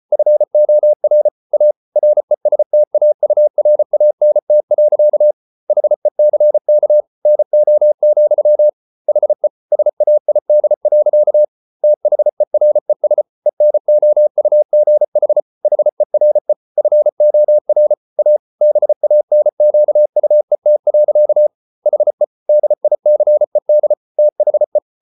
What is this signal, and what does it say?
Morse code, heard as a keyed tone, about 34 wpm, 600 Hz. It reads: FOR A RESTAURANT. HECK NO, HE SAID. THERES ENOUGH HERE FOR A BANQUET. HE DICED THE